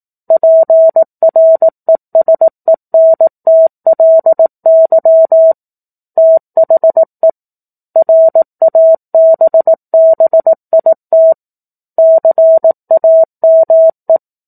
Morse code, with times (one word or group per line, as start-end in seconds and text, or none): 0.3-5.5 PRESENTLY
6.2-7.3 THE
8.0-11.3 RABBIT
12.0-14.2 CAME